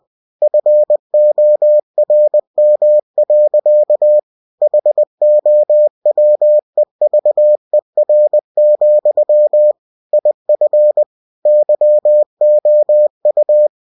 Morse code, heard as a keyed tone, about 20 wpm, 600 Hz. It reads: FORM. HOWEVER, IF YOU
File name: cw-books / winnie-the-pooh / winnie-20wpm-page-0116